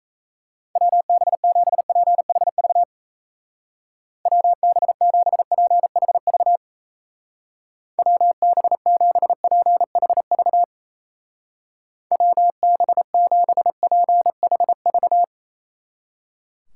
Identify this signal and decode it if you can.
Morse code, heard as a keyed tone, about 36 wpm, 700 Hz. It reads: WB7PHV WB7PHV WB7PHV WB7PHV